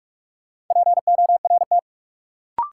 Morse code, heard as a keyed tone, about 46 wpm, 700 Hz. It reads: PORT
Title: Sentences from Top 1000 Words - Review Part 1 - New Word Drill - 45wpm